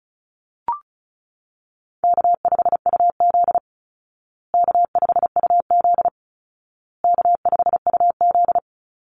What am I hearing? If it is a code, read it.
Morse code, heard as a keyed tone, about 34 wpm, 700 Hz. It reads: K5UZ K5UZ K5UZ
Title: Call Signs Repeated 2x - 35wpm